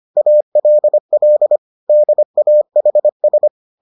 Morse code, heard as a keyed tone, about 25 wpm, 600 Hz. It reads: ALL DAHS